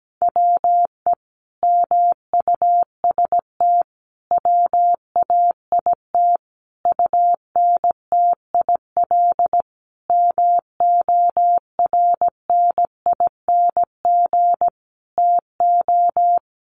Morse code, heard as a keyed tone, about 17 wpm, 700 Hz. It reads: WE MUST WAIT UNTIL MORNING TO
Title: Sentences from Top 1300 Words 17wpm